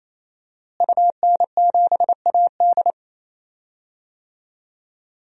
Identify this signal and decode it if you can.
Morse code, heard as a keyed tone, about 27 wpm, 700 Hz. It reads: UN7AD
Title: DXCC-100 Call Signs - 5 Characters - Repeated 3x - 28wpm